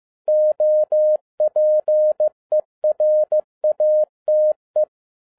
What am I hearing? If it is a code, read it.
Morse code, heard as a keyed tone, about 15 wpm, 610 Hz. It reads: OPERATE
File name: 2011-07-mp3 / n22 QRP OTAKU 15wpm